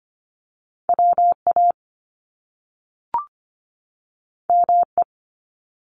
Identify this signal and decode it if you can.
Morse code, heard as a keyed tone, about 25 wpm, 700 Hz. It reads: WA ME
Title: Top Two Letter Combinations in Words 25wpm